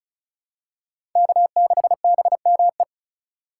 Morse code, heard as a keyed tone, about 36 wpm, 690 Hz. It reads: K6BME
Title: USA Call Signs - 5 Characters - 3x Character Spacing - 35wpm